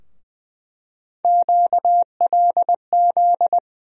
Morse code, heard as a keyed tone, about 20 wpm, 700 Hz. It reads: QLZ
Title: USA Call Sign Postfixes - 3 Letters 20wpm